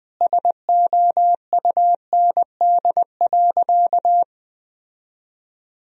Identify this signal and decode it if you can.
Morse code, heard as a keyed tone, about 20 wpm, 700 Hz. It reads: SOUND.
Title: Sentences from Top 1000 Words - Review Part 1 - 20wpm